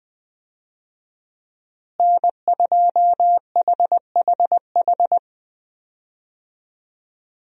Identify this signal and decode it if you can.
Morse code, heard as a keyed tone, about 21 wpm, 700 Hz. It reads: N2HHH
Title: USA Call Signs - 5 Characters - Repeated 2x - 20wpm